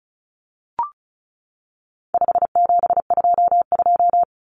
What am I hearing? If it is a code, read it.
Morse code, heard as a keyed tone, about 34 wpm, 700 Hz. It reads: 5722
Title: Common Cut Numbers - 4-Digit 35wpm